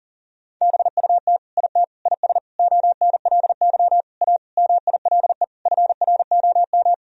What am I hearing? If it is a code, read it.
Morse code, heard as a keyed tone, about 40 wpm, 700 Hz. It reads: BUT IT IS ONLY A MILE FROM
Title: Sentences from Top 1000 Words - Review Part 1 - 40wpm